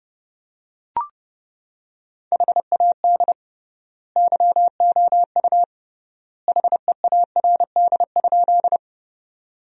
Morse code, heard as a keyed tone, about 32 wpm, 700 Hz. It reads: HAD YOU HEARD?
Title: 3x Word-Spacing - Sentences from Top 400 Words - 30wpm